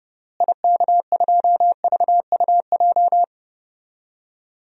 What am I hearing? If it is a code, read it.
Morse code, heard as a keyed tone, about 30 wpm, 700 Hz. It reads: IK2VUJ
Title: DXCC-100 Call Signs - 6 Characters - Repeated 3x - 30wpm